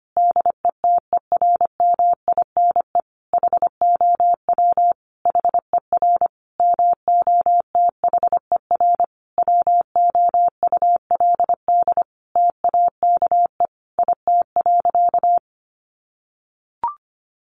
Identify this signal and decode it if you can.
Morse code, heard as a keyed tone, about 25 wpm, 700 Hz. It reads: DETERMINE HOW HER MOTHER WOULD TAKE IT.
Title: Sentences from Top 700 Words 25wpm